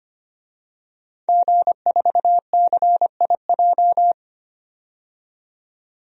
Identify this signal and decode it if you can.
Morse code, heard as a keyed tone, about 25 wpm, 700 Hz. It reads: G4CIJ
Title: DXCC-100 Call Signs - 5 Characters - 25wpm